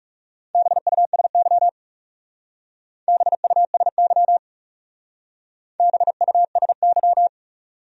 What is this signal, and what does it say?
Morse code, heard as a keyed tone, about 37 wpm, 700 Hz. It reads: BUSY BUSY BUSY
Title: Speed-Racing Top 500 Words 30wpm